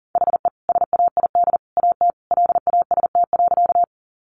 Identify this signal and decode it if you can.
Morse code, heard as a keyed tone, about 40 wpm, 700 Hz. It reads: HE SAID AT LAST.
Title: Sentences from Top 300 Words 40wpm